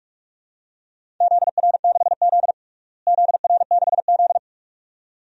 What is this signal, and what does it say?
Morse code, heard as a keyed tone, about 46 wpm, 700 Hz. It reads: ZR6Z ZR6Z
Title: DXCC-100 Call Signs - 4 Characters - Repeated 2x - 45wpm